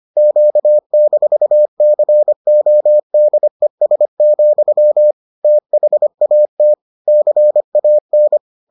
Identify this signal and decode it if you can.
Morse code, heard as a keyed tone, about 25 wpm, 600 Hz. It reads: Q-CODES, THAT CAN